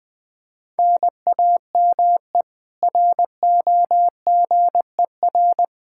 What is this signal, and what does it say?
Morse code, heard as a keyed tone, about 21 wpm, 700 Hz. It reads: NAME ROGER
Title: Practice QSO - Exchange 1 - Beginning - 20wpm